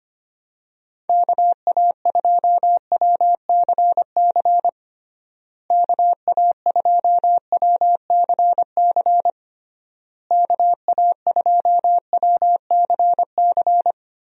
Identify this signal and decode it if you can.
Morse code, heard as a keyed tone, about 25 wpm, 700 Hz. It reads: KA2WCC KA2WCC KA2WCC
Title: USA Call Signs - 6 Characters - Repeated 3x - 25wpm